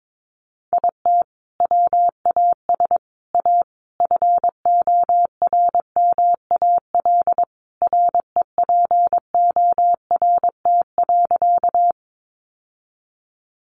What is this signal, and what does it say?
Morse code, heard as a keyed tone, about 22 wpm, 700 Hz. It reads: IT WAS A FORMAL REPORT.
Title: Sentences from Top 1300 Words 22wpm